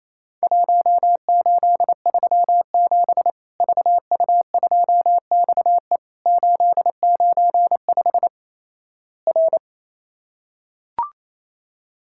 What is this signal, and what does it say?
Morse code, heard as a keyed tone, about 28 wpm, 700 Hz. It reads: 1837 VU2XE 895 R
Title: Worked All Europe DX Contest - QTC Exchange - 28wpm